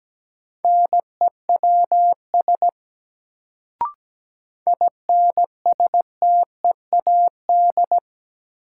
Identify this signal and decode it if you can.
Morse code, heard as a keyed tone, about 17 wpm, 700 Hz. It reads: NEWS INSTEAD
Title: Sentences from Top 800 Words - New Word Drill - 17wpm